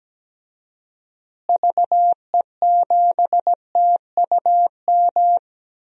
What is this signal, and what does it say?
Morse code, heard as a keyed tone, about 17 wpm, 700 Hz. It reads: VE7TUM
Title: DXCC-100 Call Signs - 6 Characters - Repeated 2x - 17wpm